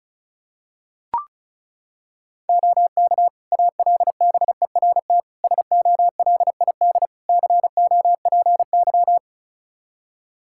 Morse code, heard as a keyed tone, about 34 wpm, 700 Hz. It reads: OK ALBERT SOLID COPY
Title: Practice QSO - Exchange 3 - More Info - 35wpm